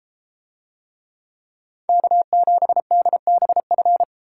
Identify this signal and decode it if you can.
Morse code, heard as a keyed tone, about 33 wpm, 700 Hz. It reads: K7DBF